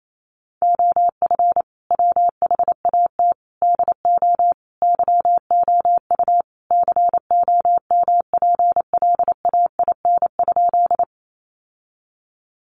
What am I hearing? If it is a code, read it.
Morse code, heard as a keyed tone, about 28 wpm, 700 Hz. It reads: OF WHAT DO YOU COMPLAIN?
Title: Sentences from Top 1300 Words 28wpm